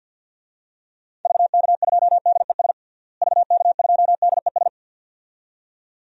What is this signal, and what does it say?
Morse code, heard as a keyed tone, about 53 wpm, 700 Hz. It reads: VK2DES VK2DES
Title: DXCC-100 Call Signs - 6 Characters - Repeated 2x - 50wpm